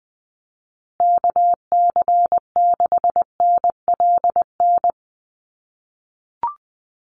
Morse code, heard as a keyed tone, about 20 wpm, 700 Hz. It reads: KC6NLN
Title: USA Call Signs Repeated 3x 20wpm